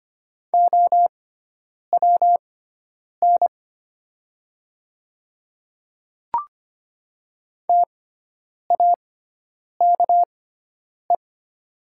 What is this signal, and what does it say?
Morse code, heard as a keyed tone, about 25 wpm, spaced as for 9 wpm, 700 Hz. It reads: OWN TAKE
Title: 6x Character Spacing - Top 300 Words - 25wpm